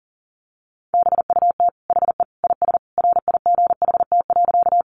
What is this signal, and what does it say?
Morse code, heard as a keyed tone, about 40 wpm, 700 Hz. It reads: BUT HE IS RIGHT.
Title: Sentences from Top 200 Words 40wpm